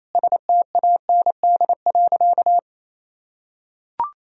Morse code, heard as a keyed tone, about 28 wpm, 700 Hz. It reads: STAND.
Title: Sentences from Top 1000 Words - Review Part 2 - 28wpm